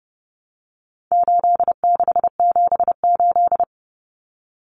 Morse code, heard as a keyed tone, about 30 wpm, 700 Hz. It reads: 8678
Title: Numbers - 4-Digit 30wpm